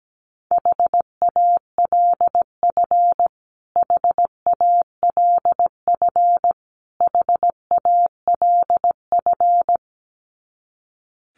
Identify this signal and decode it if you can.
Morse code, heard as a keyed tone, about 17 wpm, 700 Hz. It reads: HALF HALF HALF
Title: Top 500 Words Repeated 17wpm